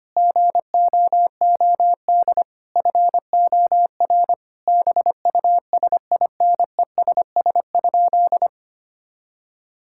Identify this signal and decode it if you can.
Morse code, heard as a keyed tone, about 25 wpm, 700 Hz. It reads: GOOD FOR BUSINESS?
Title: Sentences from Top 1000 Words - Review Part 6 - 25wpm